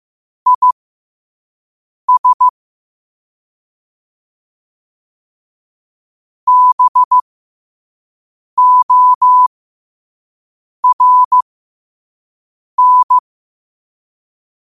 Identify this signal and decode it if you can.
Morse code, heard as a keyed tone, about 15 wpm, spaced as for 5 wpm, 1000 Hz. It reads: IS BORN
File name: qotdfiles / 5wpm